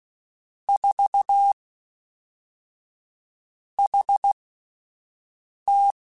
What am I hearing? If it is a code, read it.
Morse code, heard as a keyed tone, about 16 wpm, spaced as for 4 wpm, 800 Hz. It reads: 4HT
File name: CW Comparisons / CW Comparisons-mp3s / CW Comparisons Broken Characters @10